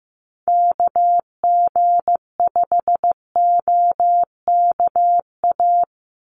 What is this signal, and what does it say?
Morse code, heard as a keyed tone, about 15 wpm, 700 Hz. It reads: KG5OKA
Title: USA Call Signs 15wpm